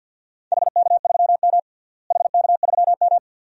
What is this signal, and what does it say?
Morse code, heard as a keyed tone, about 47 wpm, 700 Hz. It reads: HK3M HK3M
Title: DXCC-100 Call Signs - 4 Characters - Repeated 2x - 50wpm